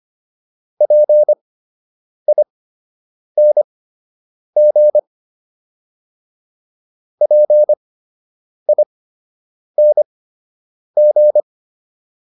Morse code, heard as a keyed tone, about 25 wpm, spaced as for 8 wpm, 600 Hz. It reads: PING PING